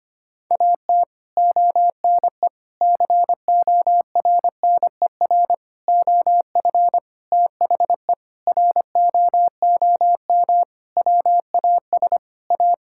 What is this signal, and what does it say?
Morse code, heard as a keyed tone, about 25 wpm, 700 Hz. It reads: AT ONE CORNER OF THE ROOM WAS A
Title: Sentences from Top 1600 Words 25wpm